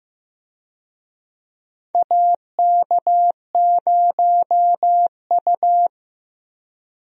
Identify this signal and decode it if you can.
Morse code, heard as a keyed tone, about 16 wpm, 700 Hz. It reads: AK0U
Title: USA Call Signs - 4 Characters - Repeated 3x - 15wpm